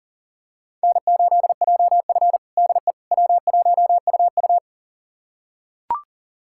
Morse code, heard as a keyed tone, about 40 wpm, 700 Hz. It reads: N8JF DE W1UU
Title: USA Callsign DE Callsign - 4 Characters - 40wpm